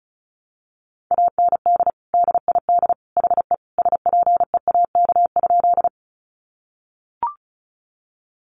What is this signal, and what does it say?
Morse code, heard as a keyed tone, about 34 wpm, 700 Hz. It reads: AND DID HE SPEAK?
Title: Sentences from Top 1100 Words 35wpm